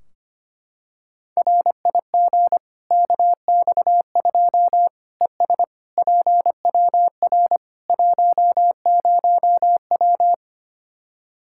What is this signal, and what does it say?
Morse code, heard as a keyed tone, about 25 wpm, 700 Hz. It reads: RIG KX2 ES PWR 10W
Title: Practice QSO - Exchange 2 - Info - 25wpm